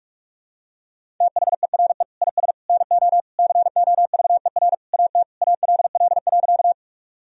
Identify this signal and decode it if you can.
Morse code, heard as a keyed tone, about 47 wpm, 700 Hz. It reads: THERE IS NO COVER AT ALL.
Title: Sentences from Top 1000 Words - Review Part 1 - 45wpm